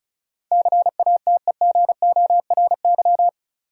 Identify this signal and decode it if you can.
Morse code, heard as a keyed tone, about 36 wpm, 700 Hz. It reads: CATEGORY
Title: Sentences from Top 1000 Words - Review Part 7 - New Word Drill - 35wpm